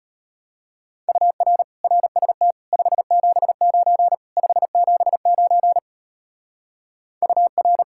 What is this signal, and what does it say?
Morse code, heard as a keyed tone, about 30 wpm, 700 Hz. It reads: UR RST 579 579 UR
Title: Speed-Racing Common QSO Phrases Encore - 25wpm